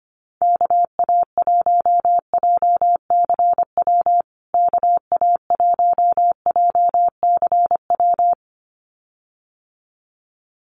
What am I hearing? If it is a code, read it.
Morse code, heard as a keyed tone, about 25 wpm, 700 Hz. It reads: KA1JCW KA1JCW
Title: USA Call Signs Repeated 2x 25wpm